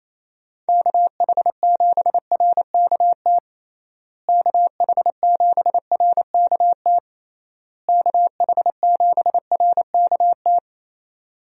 Morse code, heard as a keyed tone, about 28 wpm, 700 Hz. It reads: KH7RKT KH7RKT KH7RKT
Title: USA Call Signs - 6 Characters - Repeated 3x - 28wpm